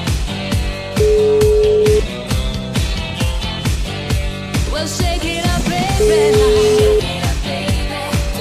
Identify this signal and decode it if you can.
Ringback tone (Europe, 425 Hz), with noise.